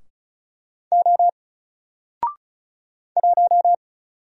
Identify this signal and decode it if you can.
Morse code, heard as a keyed tone, about 36 wpm, 700 Hz. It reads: O 1
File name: Single-Letter-Number-35wpm